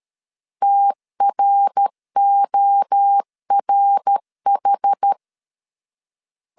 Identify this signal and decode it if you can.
Morse code, heard as a keyed tone, about 13 wpm, 800 Hz. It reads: TRORH
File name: CW / DET / 1 3 1500C S C 12wpm 2750